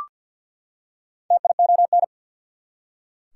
Morse code, heard as a keyed tone, about 46 wpm, 690 Hz. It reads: TION
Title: Prefixes and Suffixes 50wpm